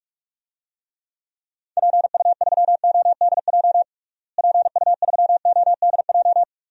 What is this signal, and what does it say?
Morse code, heard as a keyed tone, about 46 wpm, 700 Hz. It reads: PU3ODJ PU3ODJ
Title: DXCC-100 Call Signs - 6 Characters - Repeated 3x - 45wpm